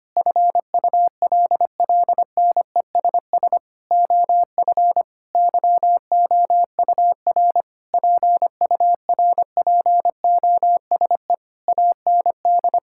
Morse code, heard as a keyed tone, about 25 wpm, 700 Hz. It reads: FULLNESS OF YOUR PURPOSE AND
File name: practice / 25wpm / Sentences with 9-Word Phrases - 25wpm